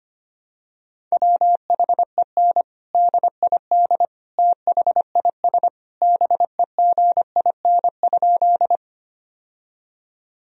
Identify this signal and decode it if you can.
Morse code, heard as a keyed tone, about 25 wpm, 700 Hz. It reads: WHEN DID THIS BEGIN?
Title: Sentences from Top 1000 Words - Review Part 1 - 25wpm